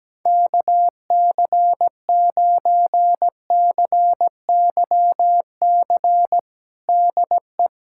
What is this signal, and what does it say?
Morse code, heard as a keyed tone, about 17 wpm, 700 Hz. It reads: KC9CYC DE
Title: USA Callsign DE Callsign 17wpm